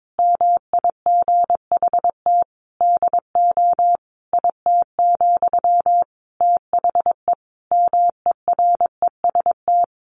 Morse code, heard as a keyed tone, about 22 wpm, 700 Hz. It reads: MIGHT DO IT, THE MEREST
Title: A Sound Of Thunder 22wpm